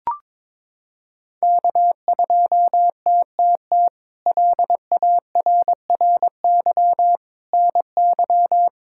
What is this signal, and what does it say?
Morse code, heard as a keyed tone, about 22 wpm, 700 Hz. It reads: K2TTT LARRY NY
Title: K1USN SST QSO Practice - 22wpm